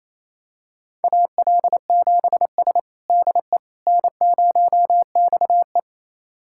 Morse code, heard as a keyed tone, about 28 wpm, 700 Hz. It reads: AL7S DE N0XE
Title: USA Callsign DE Callsign - 4 Characters - 28wpm